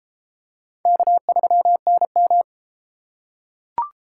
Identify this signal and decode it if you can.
Morse code, heard as a keyed tone, about 30 wpm, 700 Hz. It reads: K3NM